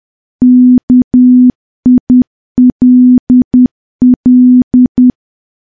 Morse code, heard as a keyed tone, about 10 wpm, 250 Hz. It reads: KILL
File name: morse / morse-training-10WPM